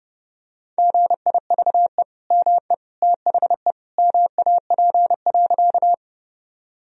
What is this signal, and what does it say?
Morse code, heard as a keyed tone, about 28 wpm, 700 Hz. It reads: GIVE ME THE MAP.
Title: Sentences from Top 1000 Words - Review Part 2 - 30wpm